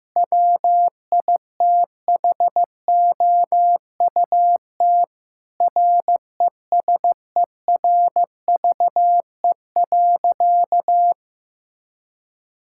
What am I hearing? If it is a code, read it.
Morse code, heard as a keyed tone, about 15 wpm, 700 Hz. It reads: WITHOUT RESERVE.
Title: Sentences from Top 1000 Words - Review Part 6 - 15wpm